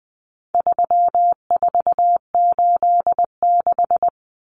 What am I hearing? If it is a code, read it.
Morse code, heard as a keyed tone, about 20 wpm, 700 Hz. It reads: 3486
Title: Common Cut Numbers - 4-Digit 20wpm